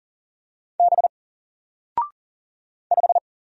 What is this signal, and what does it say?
Morse code, heard as a keyed tone, about 40 wpm, 700 Hz. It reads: B 5
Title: Single Letter-Number 40wpm